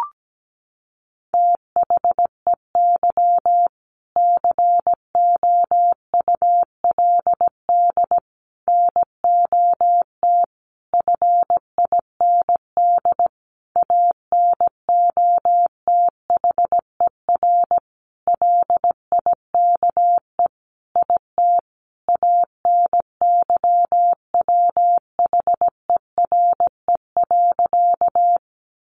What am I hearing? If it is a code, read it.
Morse code, heard as a keyed tone, about 17 wpm, 700 Hz. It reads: THEY COULD NOT FIND ANOTHER LIKE IT ANYWHERE.
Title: Sentences from Top 1400 Words 17wpm